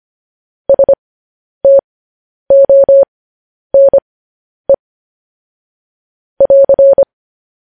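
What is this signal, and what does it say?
Morse code, heard as a keyed tone, about 25 wpm, spaced as for 10 wpm, 550 Hz. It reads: STONE <AR>
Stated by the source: News headlines sent in Morse code at 10 wpm.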